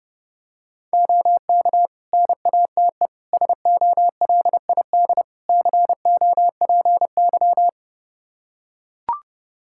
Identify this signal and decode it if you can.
Morse code, heard as a keyed tone, about 30 wpm, 700 Hz. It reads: OK NATE SOLID COPY
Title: Practice QSO - Exchange 3 - More Info - 30wpm